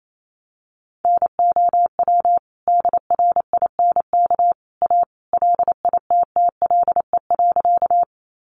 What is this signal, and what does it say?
Morse code, heard as a keyed tone, about 28 wpm, 700 Hz. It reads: NOW DRINK A LITTLE.
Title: Sentences from Top 1500 Words 28wpm